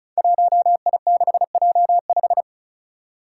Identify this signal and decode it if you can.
Morse code, heard as a keyed tone, about 36 wpm, 700 Hz. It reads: 1I6J5
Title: Five Challenging Letter-Number Combinations 35wpm